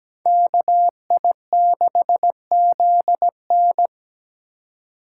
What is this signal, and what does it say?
Morse code, heard as a keyed tone, about 20 wpm, 700 Hz. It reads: KI6ZN
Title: Speed Racing - USA 2x2 Call Signs 17wpm